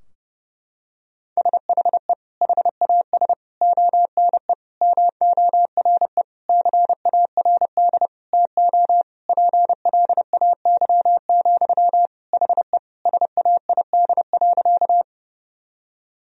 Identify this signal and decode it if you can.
Morse code, heard as a keyed tone, about 30 wpm, 700 Hz. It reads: SHE HAS ONE MORE CARD TO PLAY, HE SAID.
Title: Sentences from Top 1000 Words - Review Part 4 - 30wpm